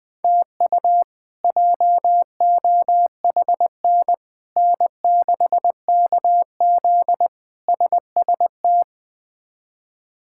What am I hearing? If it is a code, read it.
Morse code, heard as a keyed tone, about 21 wpm, 700 Hz. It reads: TU JOHN N6KZ SST
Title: K1USN SST QSO Practice - 20wpm